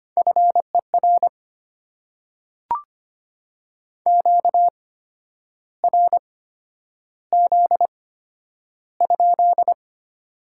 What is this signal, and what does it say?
Morse code, heard as a keyed tone, about 25 wpm, spaced as for 10 wpm, 700 Hz. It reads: FER QRZ?
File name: Top 90 QSO Elements - 8x Character Spacing - 25wpm